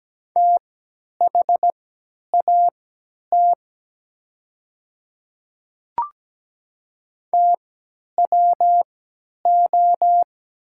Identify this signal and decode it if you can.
Morse code, heard as a keyed tone, about 17 wpm, spaced as for 10 wpm, 700 Hz. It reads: THAT TWO
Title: 3x Character Spacing - Top 300 Words - 17wpm